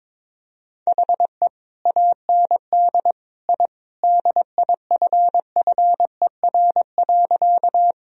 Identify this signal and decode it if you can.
Morse code, heard as a keyed tone, about 22 wpm, 700 Hz. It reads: HE AND I DIFFER.
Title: Sentences from Top 1000 Words - Review Part 1 - 22wpm